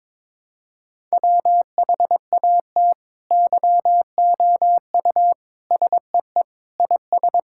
Morse code, heard as a keyed tone, about 22 wpm, 700 Hz. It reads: WHAT YOU SEE IS
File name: practice / 22wpm / Sets of 7 Words - 22wpm